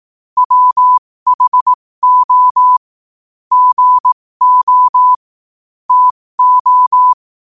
Morse code, heard as a keyed tone, about 18 wpm, spaced as for 15 wpm, 1000 Hz. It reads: WHO GO TO
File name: qotdfiles / 15wpm